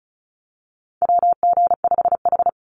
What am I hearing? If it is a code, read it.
Morse code, heard as a keyed tone, about 34 wpm, 700 Hz. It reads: WG5H
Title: Call Signs - 35wpm